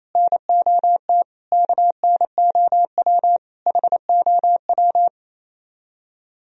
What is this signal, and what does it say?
Morse code, heard as a keyed tone, about 33 wpm, 700 Hz. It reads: NOT KNOW HOW
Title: Speed-Racing - Sets of 5 Words 28wpm